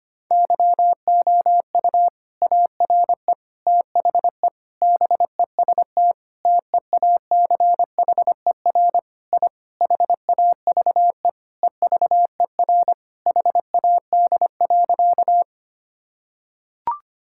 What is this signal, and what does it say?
Morse code, heard as a keyed tone, about 25 wpm, 700 Hz. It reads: YOU ARE THE BEST TEACHER I HAVE EVER HAD.
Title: Sentences from Top 1000 Words - Review Part 6 - 25wpm